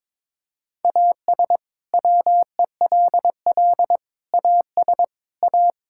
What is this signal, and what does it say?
Morse code, heard as a keyed tone, about 22 wpm, 700 Hz. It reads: AS WELL AS A
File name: practice / 22wpm / Sets of 6 Words - 22wpm